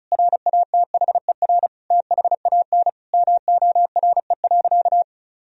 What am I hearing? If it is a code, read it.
Morse code, heard as a keyed tone, about 34 wpm, 700 Hz. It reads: RATHER THAN MORE.
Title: Sentences from Top 1000 Words - Review Part 4 - 35wpm